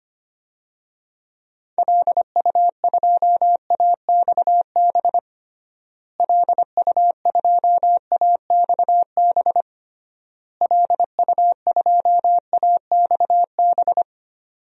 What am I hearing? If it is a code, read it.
Morse code, heard as a keyed tone, about 25 wpm, 700 Hz. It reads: LU2AXB LU2AXB LU2AXB
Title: DXCC-100 Call Signs - 6 Characters - Repeated 3x - 25wpm